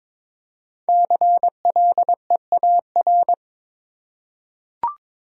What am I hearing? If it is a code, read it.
Morse code, heard as a keyed tone, about 23 wpm, 700 Hz. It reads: CLEAR
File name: 5 Letter Words - First 3 Letters Given - 22wpm